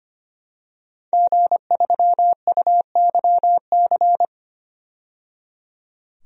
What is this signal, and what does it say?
Morse code, heard as a keyed tone, about 25 wpm, 700 Hz. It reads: G3UYC
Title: DXCC-100 Call Signs - 5 Characters - 25wpm